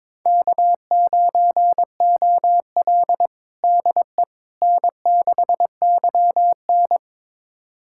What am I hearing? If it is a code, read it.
Morse code, heard as a keyed tone, about 22 wpm, 700 Hz. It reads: K9OL DE N6YN
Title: USA Callsign DE Callsign - 4 Characters - 22wpm